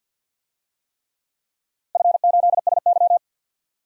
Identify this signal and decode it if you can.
Morse code, heard as a keyed tone, about 46 wpm, 700 Hz. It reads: F8SY
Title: DXCC-100 Call Signs - 4 Characters - 50wpm